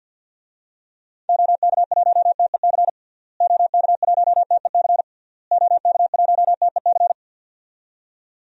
Morse code, heard as a keyed tone, about 52 wpm, 700 Hz. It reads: OK1TEC OK1TEC OK1TEC
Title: DXCC-100 Call Signs - 6 Characters - Repeated 3x - 50wpm